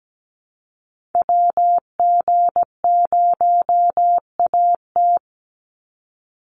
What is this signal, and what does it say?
Morse code, heard as a keyed tone, about 17 wpm, 700 Hz. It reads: WG0AT
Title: NA SOTA Activator Callsigns 17wpm